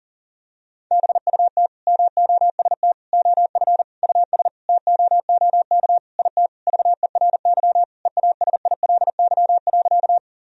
Morse code, heard as a keyed tone, about 40 wpm, 700 Hz. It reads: BUT MOST OF US TOOK IT VERY EASILY.
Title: Sentences from Top 1000 Words - Review Part 1 - 40wpm